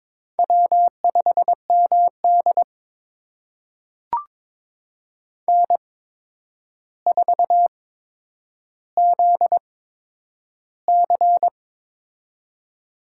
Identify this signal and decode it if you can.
Morse code, heard as a keyed tone, about 22 wpm, spaced as for 9 wpm, 700 Hz. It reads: W5MD N4ZC
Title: USA Call Signs - 8x Character Spacing - 22wpm